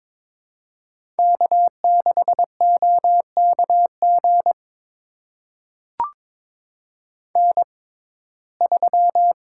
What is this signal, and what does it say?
Morse code, heard as a keyed tone, about 22 wpm, spaced as for 16 wpm, 700 Hz. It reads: K6OKG N3
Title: USA Call Signs - 6x Character Spacing - 22wpm